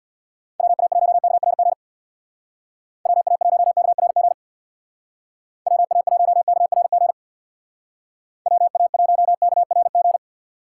Morse code, heard as a keyed tone, about 59 wpm, 700 Hz. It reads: WA1KRG WA1KRG WA1KRG WA1KRG
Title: Speed Racing - USA 2x3 Call Signs 50wpm